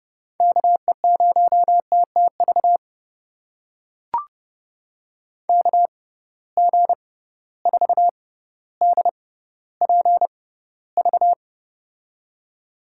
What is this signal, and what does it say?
Morse code, heard as a keyed tone, about 30 wpm, spaced as for 15 wpm, 700 Hz. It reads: KE0TTV KG4DPV
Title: USA Call Signs - 6x Character Spacing - 30wpm